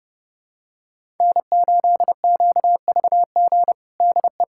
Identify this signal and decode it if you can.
Morse code, heard as a keyed tone, about 30 wpm, 700 Hz. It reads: N8QVG DE
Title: USA Callsign DE Callsign - 5 Characters - 30wpm